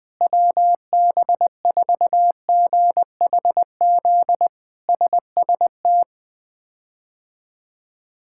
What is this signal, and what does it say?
Morse code, heard as a keyed tone, about 20 wpm, 700 Hz. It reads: WB4GHZ SST
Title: K1USN SST QSO Practice - 20wpm